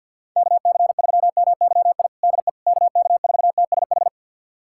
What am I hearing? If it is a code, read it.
Morse code, heard as a keyed tone, about 48 wpm, 700 Hz. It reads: KC3MYI DE KK4TSH
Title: USA Callsign DE Callsign - 6 Characters - 50wpm